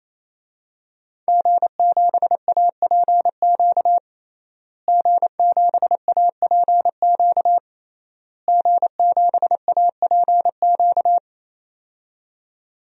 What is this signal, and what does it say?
Morse code, heard as a keyed tone, about 28 wpm, 700 Hz. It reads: G7APQ G7APQ G7APQ
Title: DXCC-100 Call Signs - 5 Characters - Repeated 3x - 28wpm